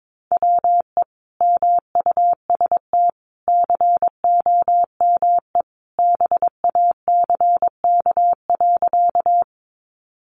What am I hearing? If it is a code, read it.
Morse code, heard as a keyed tone, about 22 wpm, 700 Hz. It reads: WE MUST COME BACK.
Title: Sentences from Top 200 Words 22wpm